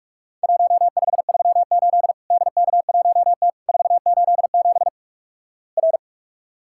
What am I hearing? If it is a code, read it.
Morse code, heard as a keyed tone, about 46 wpm, 700 Hz. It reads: 1538 DK1T 487 R
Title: Worked All Europe DX Contest - QTC Exchange - 45wpm